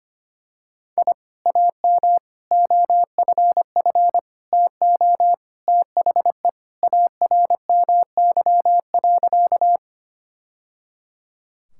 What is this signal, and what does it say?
Morse code, heard as a keyed tone, about 25 wpm, 700 Hz. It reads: I AM OFF TO THE ARMY.
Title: Sentences from Top 1000 Words - Review Part 5 - 25wpm